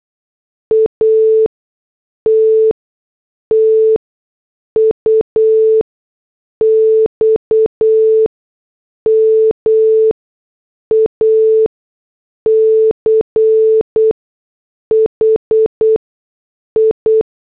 Morse code, heard as a keyed tone, about 8 wpm, spaced as for 6 wpm, 440 Hz. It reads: ATTUX MACHI